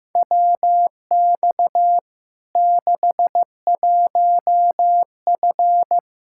Morse code, heard as a keyed tone, about 15 wpm, 700 Hz. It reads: WX 61F
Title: K1USN SST QSO Practice 15wpm